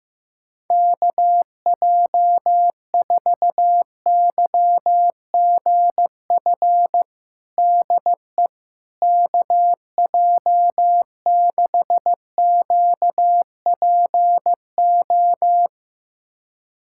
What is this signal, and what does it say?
Morse code, heard as a keyed tone, about 15 wpm, 700 Hz. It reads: KJ4YGF DE KJ6QPO
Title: USA Callsign DE Callsign 15wpm